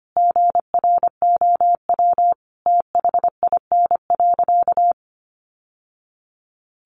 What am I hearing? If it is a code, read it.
Morse code, heard as a keyed tone, about 25 wpm, 700 Hz. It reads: GROW THIN.
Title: Sentences from Top 1200 Words 25wpm